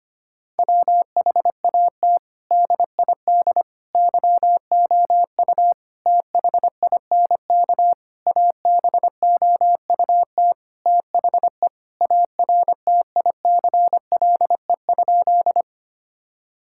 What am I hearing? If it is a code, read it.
Morse code, heard as a keyed tone, about 25 wpm, 700 Hz. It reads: WHAT DID YOU THINK ABOUT THE ARTICLE?
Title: Sentences from Top 1000 Words - Review Part 5 - 25wpm